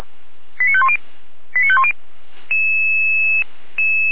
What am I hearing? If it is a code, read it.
Five-tone selective call (ZVEI), heard as quick runs of five tones, tones of about 70 ms. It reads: 89510 89510